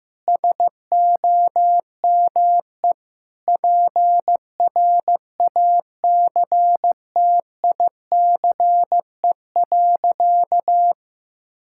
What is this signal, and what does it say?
Morse code, heard as a keyed tone, about 15 wpm, 700 Hz. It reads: SOME PRACTICE.
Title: Sentences from Top 1000 Words - Review Part 6 - 15wpm